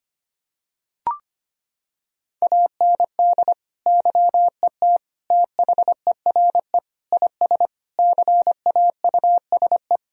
Morse code, heard as a keyed tone, about 25 wpm, 700 Hz. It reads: AND YET THERE IS CAUSE
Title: Sentences from Top 1000 Words - Review Part 1 - 25wpm